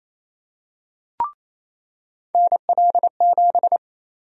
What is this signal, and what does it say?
Morse code, heard as a keyed tone, about 28 wpm, 700 Hz. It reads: NL7
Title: USA Call Sign Prefixes - 28wpm